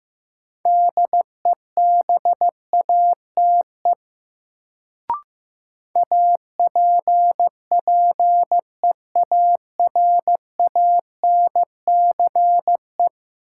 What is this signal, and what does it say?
Morse code, heard as a keyed tone, about 15 wpm, 700 Hz. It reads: DEBATE APPEARANCE
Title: Sentences from Top 1100 Words - New Word Drill Alt - 15wpm